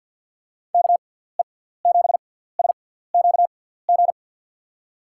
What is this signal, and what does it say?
Morse code, heard as a keyed tone, about 51 wpm, spaced as for 17 wpm, 700 Hz. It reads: KE7SQG
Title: USA Call Signs - 6 Characters - 6x Character Spacing - 50wpm